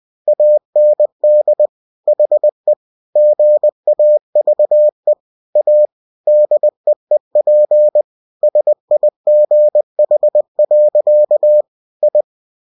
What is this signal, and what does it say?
Morse code, heard as a keyed tone, about 20 wpm, 600 Hz. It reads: AND HE GAVE A DEEP SIGH. I
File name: cw-books / winnie-the-pooh / winnie-20wpm-page-0037